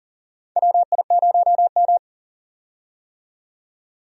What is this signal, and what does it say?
Morse code, heard as a keyed tone, about 40 wpm, 700 Hz. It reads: WI0M
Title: USA Call Signs - 4 Characters - 40wpm